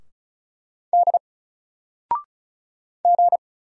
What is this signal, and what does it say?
Morse code, heard as a keyed tone, about 35 wpm, 700 Hz. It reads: D G
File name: Single-Letter-Number-35wpm